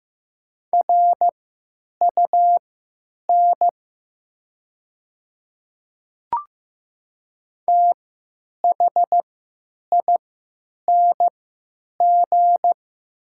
Morse code, heard as a keyed tone, about 15 wpm, spaced as for 9 wpm, 700 Hz. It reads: RUN THING
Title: 3x Character Spacing - Top 300 Words - 15wpm